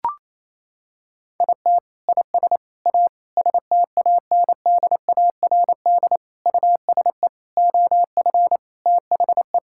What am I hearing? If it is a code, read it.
Morse code, heard as a keyed tone, about 28 wpm, 700 Hz. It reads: IT IS A STANDARD USE OF THE
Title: Sentences from Top 1000 Words - Review Part 6 - 28wpm